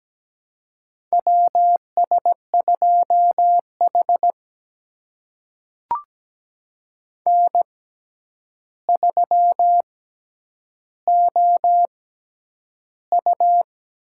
Morse code, heard as a keyed tone, about 17 wpm, spaced as for 9 wpm, 700 Hz. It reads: WS2H N3OU
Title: USA Call Signs - 6x Character Spacing - 17wpm